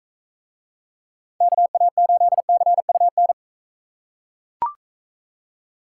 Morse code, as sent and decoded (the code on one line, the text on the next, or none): -.- .- ---.. -.-. ..- -.
KA8CUN